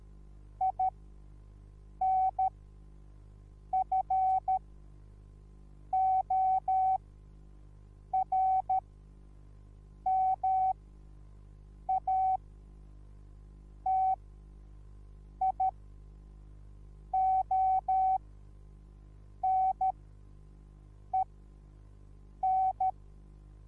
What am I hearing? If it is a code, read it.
Morse code, heard as a keyed tone, about 13 wpm, spaced as for 5 wpm, 760 Hz. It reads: INFORMATIONEN
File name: Onlinelehrgang / morsen / DJ4UF Morsen21